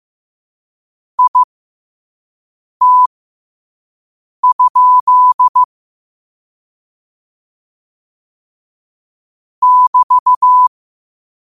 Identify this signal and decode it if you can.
Morse code, heard as a keyed tone, about 15 wpm, spaced as for 5 wpm, 1000 Hz. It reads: IT? <BT>
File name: 5wpm